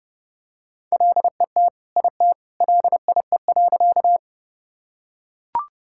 Morse code, heard as a keyed tone, about 30 wpm, 700 Hz. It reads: LET IT LIE.
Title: Sentences from Top 1600 Words 30wpm